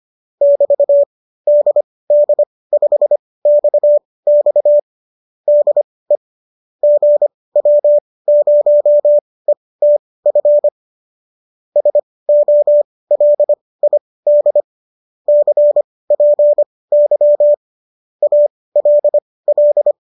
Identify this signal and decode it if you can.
Morse code, heard as a keyed tone, about 25 wpm, spaced as for 18 wpm, 590 Hz. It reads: <BT> DD5XX DE GW0ETF SOLID CPY ALL
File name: qso218 18